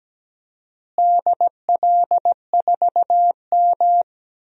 Morse code, heard as a keyed tone, about 17 wpm, 700 Hz. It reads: DL4M
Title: DXCC-100 Call Signs - 4 Characters - Repeated 2x - 17wpm